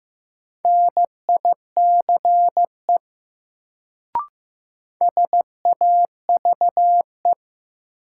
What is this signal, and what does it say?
Morse code, heard as a keyed tone, about 15 wpm, 700 Hz. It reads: NICE SAVE
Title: Sentences from Top 1100 Words - New Word Drill - 15wpm